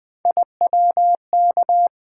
Morse code, heard as a keyed tone, about 20 wpm, 700 Hz. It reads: IWK
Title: Three Letter-Number Combinations - 20wpm